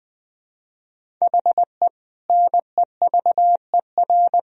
Morse code, heard as a keyed tone, about 20 wpm, 700 Hz. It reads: HE NEVER
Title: Sentences from Top 400 Words and Abbreviations 20wpm